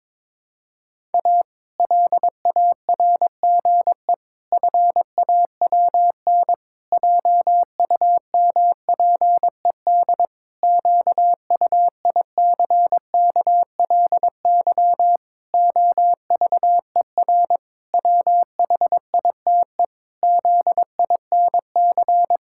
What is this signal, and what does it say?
Morse code, heard as a keyed tone, about 22 wpm, 700 Hz. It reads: A LARGE FAWN JUMPED QUICKLY OVER WHITE ZINC
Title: Pangram Challenge v2 - 22wpm